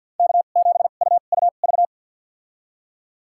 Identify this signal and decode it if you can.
Morse code, heard as a keyed tone, about 51 wpm, spaced as for 36 wpm, 700 Hz. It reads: K7UUV
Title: USA Call Signs - 5 Characters - 2x Character Spacing - 50wpm